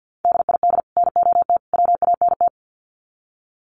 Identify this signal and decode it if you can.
Morse code, heard as a keyed tone, about 50 wpm, 700 Hz. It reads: DID NOT WANT
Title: Sets of 3 Words 50wpm